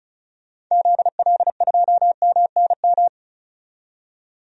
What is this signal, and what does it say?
Morse code, heard as a keyed tone, about 34 wpm, 700 Hz. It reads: ZL2MNM
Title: DXCC-100 Call Signs - 6 Characters - 35wpm